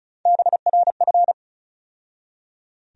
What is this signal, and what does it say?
Morse code, heard as a keyed tone, about 36 wpm, 700 Hz. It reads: BRF
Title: USA Call Sign Postfixes - 3 Letters 35wpm